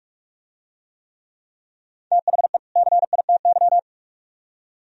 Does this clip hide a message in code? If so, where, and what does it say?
Yes: Morse code from 2.1 to 3.8 s, heard as a keyed tone, about 45 wpm, 700 Hz. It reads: THE CITY